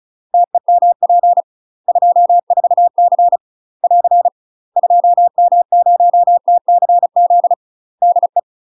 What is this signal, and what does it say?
Morse code, heard as a keyed tone, about 36 wpm, 700 Hz. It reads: TEMP 24C <AR> 2M0TCZ DE
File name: UK MASTER QSOs TRAINING/35WPM